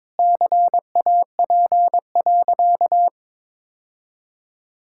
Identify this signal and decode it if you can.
Morse code, heard as a keyed tone, about 22 wpm, 700 Hz. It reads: CAP.
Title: Sentences from Top 1600 Words 22wpm